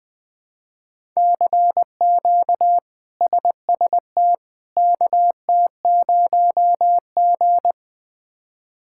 Morse code, heard as a keyed tone, about 21 wpm, 700 Hz. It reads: CQ SST KT0G
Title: K1USN SST QSO Practice - 20wpm